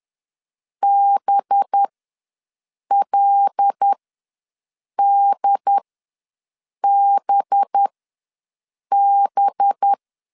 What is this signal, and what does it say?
Morse code, heard as a keyed tone, about 10 wpm, spaced as for 6 wpm, 800 Hz. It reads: BLDBB
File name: LEZIONI / 5 LBVC / 5 LB ws 10wpm 3616